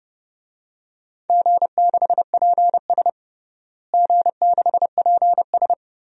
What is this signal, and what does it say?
Morse code, heard as a keyed tone, about 32 wpm, 700 Hz. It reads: G6PS G6PS
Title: DXCC-100 Call Signs - 4 Characters - Repeated 2x - 30wpm